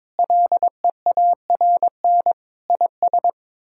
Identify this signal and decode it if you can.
Morse code, heard as a keyed tone, about 22 wpm, 700 Hz. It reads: LEARN IS
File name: practice / 22wpm / Sentences with 4-Word Phrases - 22wpm